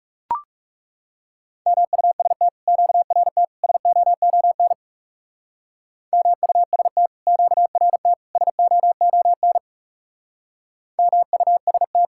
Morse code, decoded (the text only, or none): MUST QRT SOON MUST QRT SOON MUST